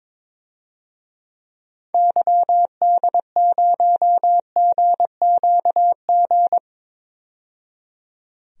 Morse code, heard as a keyed tone, about 22 wpm, 700 Hz. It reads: YD0GQG
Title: DXCC-100 Call Signs - 6 Characters - 22wpm